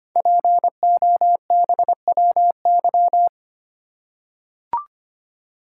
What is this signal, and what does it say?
Morse code, heard as a keyed tone, about 25 wpm, 700 Hz. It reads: POBWY
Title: Code Groups - 5 Letters - 25wpm